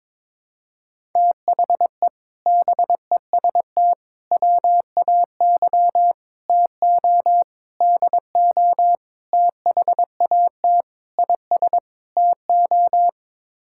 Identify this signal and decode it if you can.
Morse code, heard as a keyed tone, about 21 wpm, 700 Hz. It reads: THE BEST WAY TO DO THAT IS TO
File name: practice / 22wpm / Sets of 8 Words - 22wpm